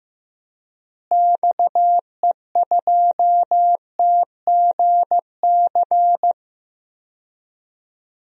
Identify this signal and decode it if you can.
Morse code, heard as a keyed tone, about 15 wpm, 700 Hz. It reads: XE2TGC
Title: DXCC-100 Call Signs - 6 Characters - Repeated 3x - 15wpm